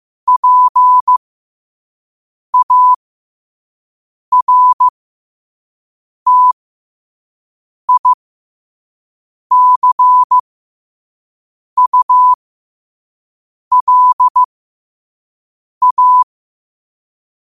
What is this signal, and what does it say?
Morse code, heard as a keyed tone, about 15 wpm, spaced as for 5 wpm, 1000 Hz. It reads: PARTICULA
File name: qotdfiles / 5wpm